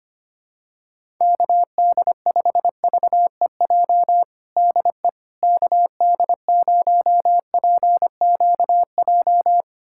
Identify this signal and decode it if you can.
Morse code, heard as a keyed tone, about 25 wpm, 700 Hz. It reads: KD5VEJ DE KD0PQJ
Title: USA Callsign DE Callsign - 6 Characters - 25wpm